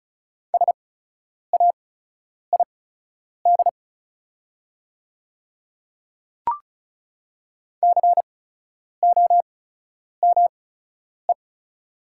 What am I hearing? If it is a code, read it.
Morse code, heard as a keyed tone, about 36 wpm, spaced as for 10 wpm, 700 Hz. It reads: SAID COME
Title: 8x Character Spacing - Top 200 Words - 35wpm